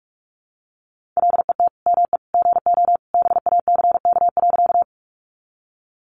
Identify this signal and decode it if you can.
Morse code, heard as a keyed tone, about 44 wpm, 700 Hz. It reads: LET ME GO BACK.
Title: Sentences from Top 300 Words 45wpm